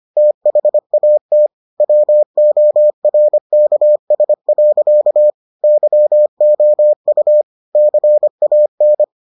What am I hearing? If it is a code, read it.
Morse code, heard as a keyed tone, about 25 wpm, 600 Hz. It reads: THAT WORKS. YOU CAN